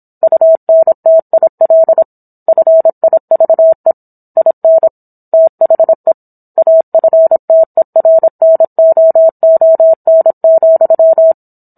Morse code, heard as a keyed tone, about 27 wpm, 650 Hz. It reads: UNTIL FIVE IN THE AFTERNOON,